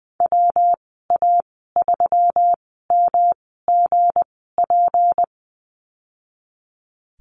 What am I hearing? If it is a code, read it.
Morse code, heard as a keyed tone, about 20 wpm, spaced as for 15 wpm, 700 Hz. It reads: WA3MGP
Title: 167 Lesson 27g - Number 7 - Call Signs 20wpm